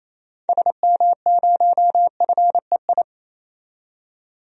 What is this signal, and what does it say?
Morse code, heard as a keyed tone, about 27 wpm, 700 Hz. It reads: SM0FEI
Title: DXCC-100 Call Signs - 6 Characters - 28wpm